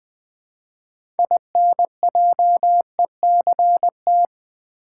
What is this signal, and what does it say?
Morse code, heard as a keyed tone, about 20 wpm, 700 Hz. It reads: INJECT
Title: Build A Word - 3 Letters At A Time - 6-Letter Words - 20wpm